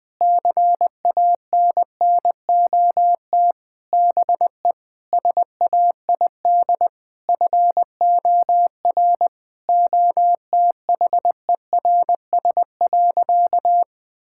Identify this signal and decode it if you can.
Morse code, heard as a keyed tone, about 21 wpm, 700 Hz. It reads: CANNOT BE SAID FOR OTHERS.
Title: Sentences with 6-Word Phrases - Spoken Beginning - 20wpm